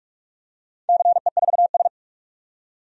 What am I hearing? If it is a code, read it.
Morse code, heard as a keyed tone, about 46 wpm, 700 Hz. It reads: CE4S
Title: DXCC-100 Call Signs - 4 Characters - 45wpm